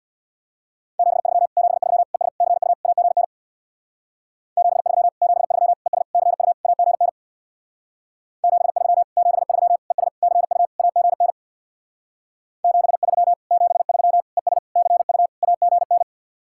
Morse code, heard as a keyed tone, about 62 wpm, 700 Hz. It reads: 73 73 ES CU AGN 73 73 ES CU AGN 73 73 ES CU AGN 73 73 ES CU AGN
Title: Speed-Racing Common QSO Phrases - 50wpm